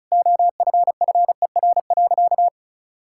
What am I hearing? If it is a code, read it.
Morse code, heard as a keyed tone, about 36 wpm, 700 Hz. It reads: OFFER.
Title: Sentences from Top 1600 Words 35wpm